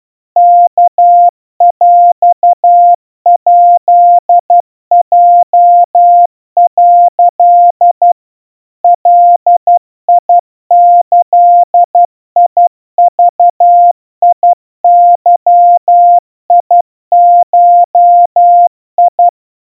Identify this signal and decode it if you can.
Morse code (Japanese Wabun code), heard as a keyed tone, about 12 wpm, 700 Hz. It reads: ワヰヱヲ」 ガギグゲゴ
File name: aiueo 40 cpm 700 Hz Crispy